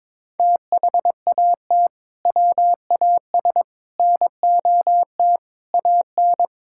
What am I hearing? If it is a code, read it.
Morse code, heard as a keyed tone, about 22 wpm, 700 Hz. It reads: THAT WAS NOT AN
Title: Sentences from Top 1000 Words - Review Part 5 - 22wpm